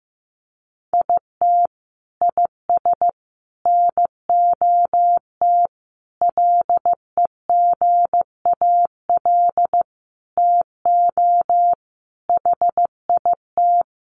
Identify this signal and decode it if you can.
Morse code, heard as a keyed tone, about 14 wpm, 700 Hz. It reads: IT IS NOT LEGAL TO HIT
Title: Sentences from Top 1200 Words 15wpm